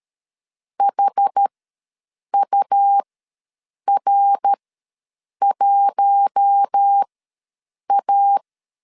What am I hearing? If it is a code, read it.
Morse code, heard as a keyed tone, about 13 wpm, spaced as for 7 wpm, 800 Hz. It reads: HUR1A
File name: 5 LB ws 12wpm